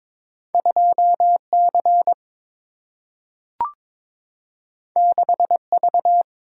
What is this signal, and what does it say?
Morse code, heard as a keyed tone, about 21 wpm, 700 Hz. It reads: 2C 6V
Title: Single-Letter Single-Number 22wpm